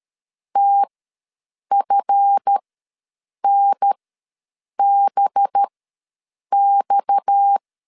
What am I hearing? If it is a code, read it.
Morse code, heard as a keyed tone, about 13 wpm, spaced as for 7 wpm, 800 Hz. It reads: TFNBX
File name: LEZIONI / 7 GF9Z / 7 GF9 ws 12wpm 3344